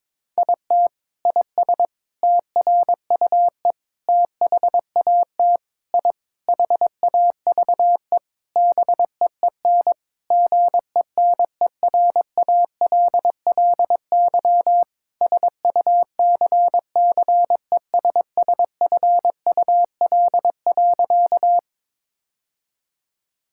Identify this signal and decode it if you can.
Morse code, heard as a keyed tone, about 22 wpm, 700 Hz. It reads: IT IS TRUE THAT I HAVE BEEN GENERALLY SUCCESSFUL.
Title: Sentences from Top 1000 Words - Review Part 6 - 22wpm